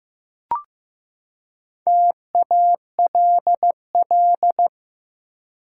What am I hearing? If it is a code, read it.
Morse code, heard as a keyed tone, about 15 wpm, 700 Hz. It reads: TALL
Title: Four Letter Words - 15wpm